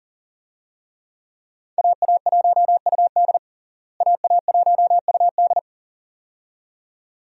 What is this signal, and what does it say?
Morse code, heard as a keyed tone, about 37 wpm, 700 Hz. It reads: AA1UD AA1UD
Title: USA Call Signs - 5 Characters - Repeated 2x - 40wpm